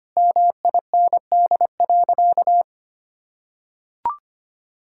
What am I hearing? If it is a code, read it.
Morse code, heard as a keyed tone, about 25 wpm, 700 Hz. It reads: MIND.
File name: practice / 25wpm / Sentences with 4-Word Phrases - 25wpm